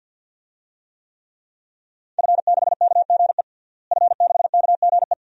Morse code, heard as a keyed tone, about 49 wpm, 700 Hz. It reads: F6KGE F6KGE
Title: DXCC-100 Call Signs - 5 Characters - Repeated 2x - 50wpm